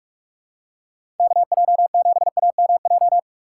Morse code, heard as a keyed tone, about 47 wpm, 700 Hz. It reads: KJ7AMJ
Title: USA Call Signs - 6 Characters - 6x Character Spacing - 45wpm